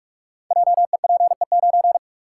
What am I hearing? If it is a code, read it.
Morse code, heard as a keyed tone, about 44 wpm, 700 Hz. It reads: JEPE9
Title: Code Groups - 5 Letter:Numbers - 45wpm